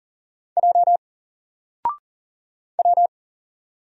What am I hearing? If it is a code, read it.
Morse code, heard as a keyed tone, about 40 wpm, 700 Hz. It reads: J W
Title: Single Letter-Number 40wpm